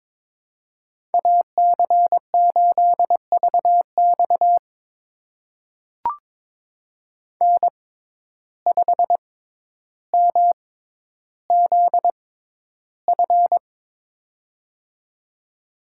Morse code, heard as a keyed tone, about 22 wpm, spaced as for 11 wpm, 700 Hz. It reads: AC8VX N5MZF
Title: USA Call Signs - 5 Characters - 6x Character Spacing - 22wpm